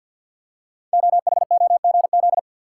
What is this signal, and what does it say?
Morse code, heard as a keyed tone, about 51 wpm, 700 Hz. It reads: OHOGZ
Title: Code Groups - 5 Letters - 50wpm